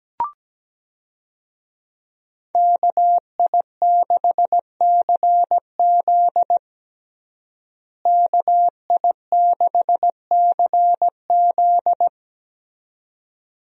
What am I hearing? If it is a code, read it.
Morse code, heard as a keyed tone, about 17 wpm, 700 Hz. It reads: KI6CZ KI6CZ
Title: USA Call Signs - 5 Characters - Repeated 2x - 17wpm